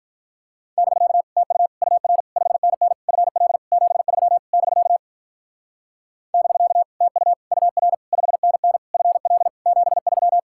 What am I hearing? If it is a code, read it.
Morse code, heard as a keyed tone, about 47 wpm, 700 Hz. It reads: <BK> TU UR 5NN FL 73 <BK> <BK> TU UR 5NN FL 73